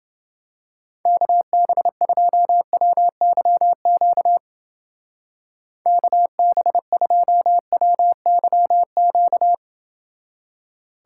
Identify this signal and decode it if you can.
Morse code, heard as a keyed tone, about 29 wpm, 700 Hz. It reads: KB2WYQ KB2WYQ